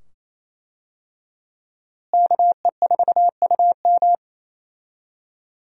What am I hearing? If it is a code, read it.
Morse code, heard as a keyed tone, about 28 wpm, 700 Hz. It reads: KE4UM
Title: USA Call Signs - 5 Characters - Repeated 2x - 28wpm